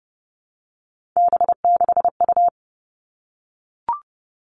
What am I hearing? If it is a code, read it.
Morse code, heard as a keyed tone, about 32 wpm, 700 Hz. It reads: B6U
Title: Three Challenging Letter-Number Combinations 30wpm